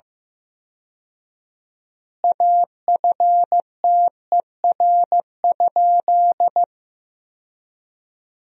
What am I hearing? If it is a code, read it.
Morse code, heard as a keyed tone, about 16 wpm, 700 Hz. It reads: AFTER?
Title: 4x Word-Spacing - Sentences from Top 200 Words - 15wpm